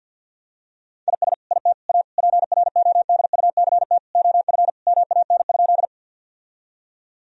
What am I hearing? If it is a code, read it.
Morse code, heard as a keyed tone, about 50 wpm, 700 Hz. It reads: IS IT A PRODUCT OF MAN?
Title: Sentences from Top 1000 Words - Review Part 2 - 50wpm